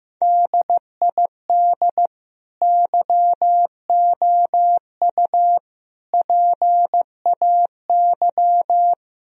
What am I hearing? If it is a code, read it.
Morse code, heard as a keyed tone, about 15 wpm, 700 Hz. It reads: DID YOU PAY
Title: Sentences from Top 1000 Words - Review Part 3 - 15wpm